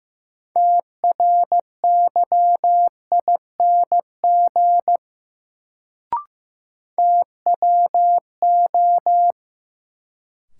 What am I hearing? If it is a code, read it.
Morse code, heard as a keyed tone, about 15 wpm, 700 Hz. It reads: TRYING TWO
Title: Sets of 2 Words - New Word Drill - 15wpm